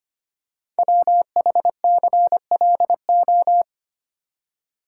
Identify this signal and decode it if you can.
Morse code, heard as a keyed tone, about 25 wpm, 700 Hz. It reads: WHCLO
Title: Code Groups - 5 Letter:Numbers - 25wpm